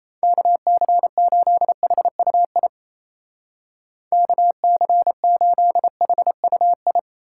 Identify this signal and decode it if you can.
Morse code, heard as a keyed tone, about 34 wpm, 700 Hz. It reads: KC8HUI KC8HUI
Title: Speed Racing - USA 2x3 Call Signs 28wpm